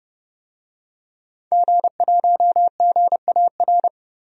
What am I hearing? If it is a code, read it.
Morse code, heard as a keyed tone, about 30 wpm, 700 Hz. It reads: G1GAR
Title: DXCC-100 Call Signs - 5 Characters - Repeated 2x - 30wpm